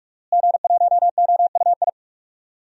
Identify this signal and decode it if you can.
Morse code, heard as a keyed tone, about 43 wpm, 700 Hz. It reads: G1OUI
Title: DXCC-100 Call Signs - 5 Characters - Repeated 2x - 45wpm